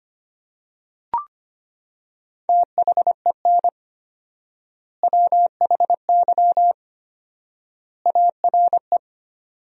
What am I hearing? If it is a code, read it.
Morse code, heard as a keyed tone, about 25 wpm, 700 Hz. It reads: THEN WHY ARE
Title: 4x Word-Spacing - Sentences from Top 200 Words - 25wpm